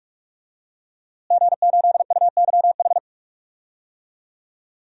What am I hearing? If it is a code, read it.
Morse code, heard as a keyed tone, about 44 wpm, 700 Hz. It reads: G8UYH
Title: DXCC-100 Call Signs - 5 Characters - Repeated 3x - 45wpm